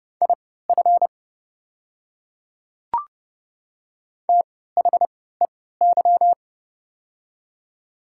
Morse code, heard as a keyed tone, about 30 wpm, spaced as for 17 wpm, 700 Hz. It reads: IF THEY
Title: 3x Character Spacing - Top 100 Words - 30wpm